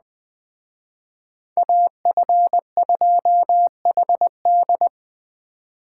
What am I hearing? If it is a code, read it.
Morse code, heard as a keyed tone, about 20 wpm, 700 Hz. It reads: AF2HD
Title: Speed Racing - USA 2x2 Call Signs 17wpm